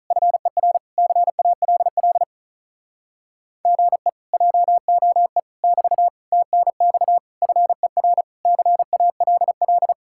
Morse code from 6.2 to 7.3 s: TNX

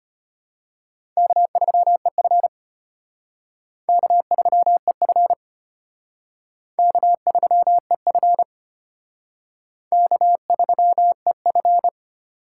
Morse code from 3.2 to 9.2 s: K3EF K3EF

